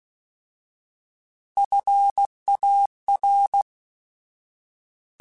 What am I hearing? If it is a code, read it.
Morse code, heard as a keyed tone, about 16 wpm, 800 Hz. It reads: FAR
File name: Alphabetical / Alphabetical mp3s / Fs @15